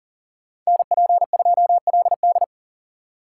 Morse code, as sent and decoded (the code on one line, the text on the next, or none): -. .--. ..--- .-.. -..
NP2LD